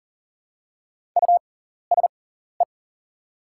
Morse code, heard as a keyed tone, about 40 wpm, spaced as for 14 wpm, 690 Hz. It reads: USE
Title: 6x Character Spacing - Top 300 Words - 40wpm